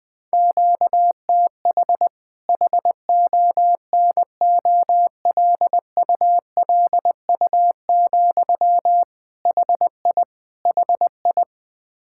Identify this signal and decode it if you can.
Morse code, heard as a keyed tone, about 20 wpm, 700 Hz. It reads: QTH HONOLULU, HI HI
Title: Practice QSO - Exchange 1 - Beginning - 20wpm